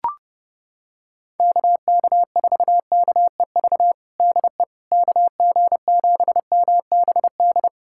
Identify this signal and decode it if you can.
Morse code, heard as a keyed tone, about 32 wpm, 700 Hz. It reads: KK4KEV DE KG7MBD
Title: USA Callsign DE Callsign - 6 Characters - 30wpm